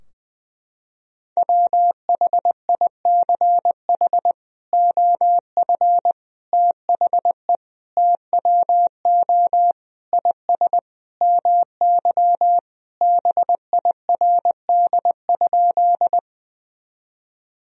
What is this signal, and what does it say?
Morse code, heard as a keyed tone, about 20 wpm, 700 Hz. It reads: WHICH OF THE TWO IS MY BIRD?
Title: Sentences from Top 1000 Words - Review Part 2 - 20wpm